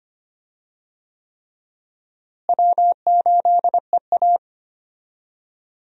Morse code, heard as a keyed tone, about 25 wpm, 700 Hz. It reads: W8EA
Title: USA Call Signs - 4 Characters - Repeated 3x - 25wpm